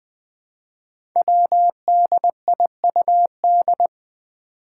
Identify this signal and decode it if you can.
Morse code, heard as a keyed tone, about 20 wpm, 700 Hz. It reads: WDIUD
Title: Code Groups - 5 Letter:Numbers - 20wpm